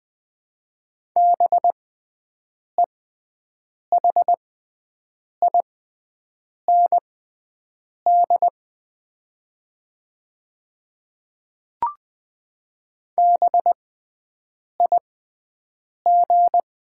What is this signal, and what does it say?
Morse code, heard as a keyed tone, about 20 wpm, spaced as for 7 wpm, 700 Hz. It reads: BEHIND BIG
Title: 6x Character Spacing - Top 500 Words - 20wpm